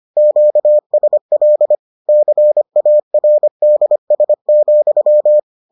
Morse code, heard as a keyed tone, about 25 wpm, 600 Hz. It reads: QSL CARDS,